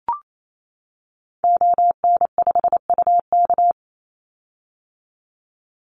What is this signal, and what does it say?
Morse code, heard as a keyed tone, about 28 wpm, 700 Hz. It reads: ON5UK
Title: Call Signs - 28wpm